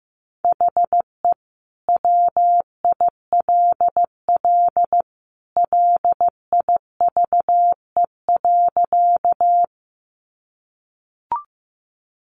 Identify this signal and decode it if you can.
Morse code, heard as a keyed tone, about 15 wpm, 700 Hz. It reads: HE WILL LIVE.
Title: Sentences from Top 200 Words 15wpm